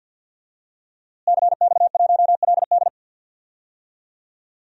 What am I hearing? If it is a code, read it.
Morse code, heard as a keyed tone, about 52 wpm, 700 Hz. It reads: CX1LD
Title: DXCC-100 Call Signs - 5 Characters - Repeated 3x - 50wpm